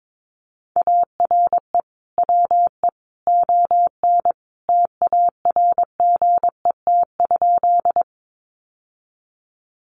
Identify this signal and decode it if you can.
Morse code, heard as a keyed tone, about 22 wpm, 700 Hz. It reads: ARE WE ON TARGET?
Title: Sentences from Top 1200 Words 22wpm